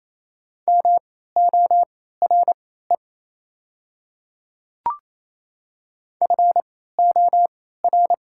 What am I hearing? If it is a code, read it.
Morse code, heard as a keyed tone, about 27 wpm, spaced as for 16 wpm, 690 Hz. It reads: MORE FOR
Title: 3x Character Spacing - Top 100 Words - 28wpm